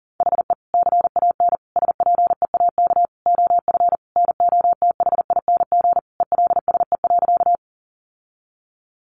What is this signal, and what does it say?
Morse code, heard as a keyed tone, about 40 wpm, 700 Hz. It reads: HE CAN SPEAK OF NOTHING ELSE.
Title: Sentences from Top 1100 Words 40wpm